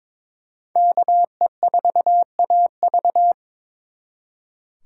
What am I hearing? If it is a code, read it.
Morse code, heard as a keyed tone, about 26 wpm, 700 Hz. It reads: KE4AV
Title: Speed Racing - USA 2x2 Call Signs 22wpm